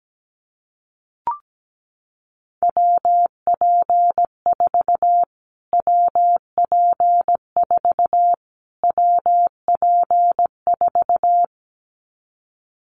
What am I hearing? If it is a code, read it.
Morse code, heard as a keyed tone, about 17 wpm, 700 Hz. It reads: WP4 WP4 WP4
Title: Callsign Prefixes for Top 25 DXCC - Repeated 3x - 17wpm